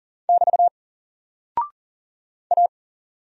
Morse code, heard as a keyed tone, about 39 wpm, 700 Hz. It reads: <BT> A
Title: Single Character (Letter:Number:Punctuation:Prosign) 40wpm